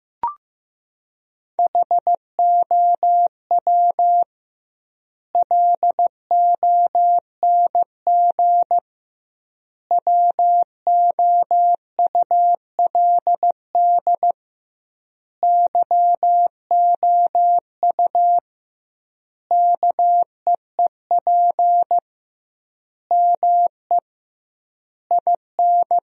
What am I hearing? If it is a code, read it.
Morse code, heard as a keyed tone, about 14 wpm, 700 Hz. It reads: HOW LONG WOULD YOU KEEP ME IN
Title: 2x Word-Spacing - Sentences from Top 500 Words - 15wpm